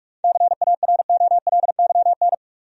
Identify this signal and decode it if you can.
Morse code, heard as a keyed tone, about 46 wpm, 700 Hz. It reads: CAROLYN
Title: USA Common Names - 45wpm